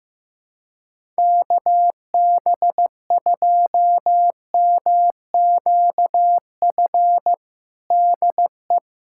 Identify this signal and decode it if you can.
Morse code, heard as a keyed tone, about 15 wpm, 700 Hz. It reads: KB2MQF DE
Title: USA Callsign DE Callsign 15wpm